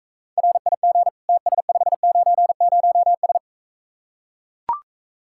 Morse code, heard as a keyed tone, about 45 wpm, 700 Hz. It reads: RIG TS590S